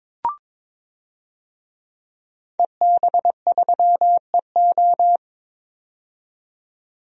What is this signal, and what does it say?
Morse code, heard as a keyed tone, about 22 wpm, 700 Hz. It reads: EB3EO
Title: DXCC-100 Call Signs - 5 Characters - 22wpm